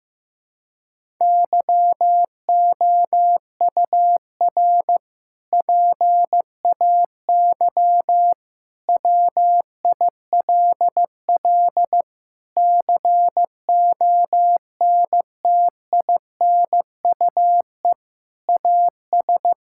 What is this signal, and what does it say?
Morse code, heard as a keyed tone, about 16 wpm, 700 Hz. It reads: YOUR PAY WILL CONTINUE AS
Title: Sentences from Top 1000 Words - Review Part 6 - 15wpm